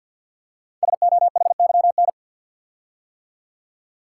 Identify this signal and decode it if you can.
Morse code, heard as a keyed tone, about 53 wpm, 700 Hz. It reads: SOHYN
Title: Code Groups - 5 Letters - 50wpm